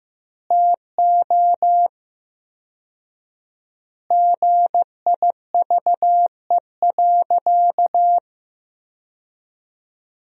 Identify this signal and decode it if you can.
Morse code, heard as a keyed tone, about 15 wpm, 700 Hz. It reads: TO GIVE.
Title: 4x Word-Spacing - Sentences from Top 200 Words - 15wpm